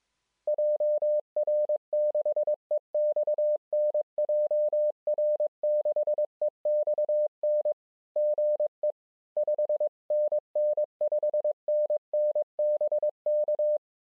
Morse code, heard as a keyed tone, about 22 wpm, 560 Hz. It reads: JR6EXN JR6EXN GE 5NN 5NN BK